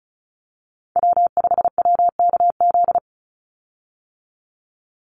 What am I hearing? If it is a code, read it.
Morse code, heard as a keyed tone, about 34 wpm, 700 Hz. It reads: W5WKZ
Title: USA Call Signs 35wpm